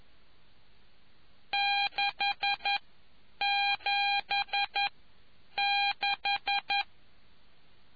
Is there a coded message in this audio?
Yes: Morse code from 1.5 to 6.8 s, heard as a keyed tone, about 11 wpm, spaced as for 8 wpm, 800 Hz. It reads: 676